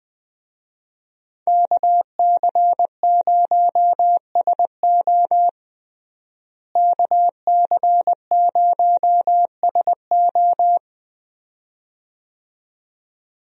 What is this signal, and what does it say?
Morse code, heard as a keyed tone, about 20 wpm, 700 Hz. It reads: KC0SO KC0SO
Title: USA Call Signs - 5 Characters - Repeated 2x - 20wpm